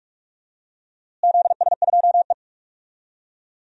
Morse code, heard as a keyed tone, about 42 wpm, 700 Hz. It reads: ZS2E
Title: DXCC-100 Call Signs - 4 Characters - Repeated 2x - 45wpm